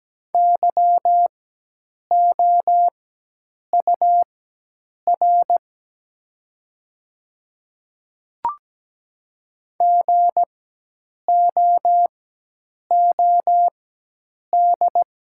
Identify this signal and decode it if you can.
Morse code, heard as a keyed tone, about 17 wpm, spaced as for 8 wpm, 700 Hz. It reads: YOUR GOOD
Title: 4x Character Spacing - Top 200 Words - 17wpm